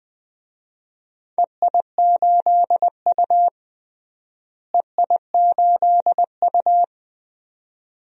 Morse code, heard as a keyed tone, about 20 wpm, 700 Hz. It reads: EI8U EI8U
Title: DXCC-100 Call Signs - 4 Characters - Repeated 2x - 20wpm